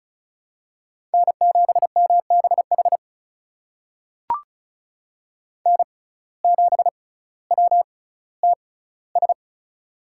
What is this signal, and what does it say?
Morse code, heard as a keyed tone, about 36 wpm, spaced as for 18 wpm, 700 Hz. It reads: N7MBH N7WTS
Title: USA Call Signs - 5 Characters - 6x Character Spacing - 35wpm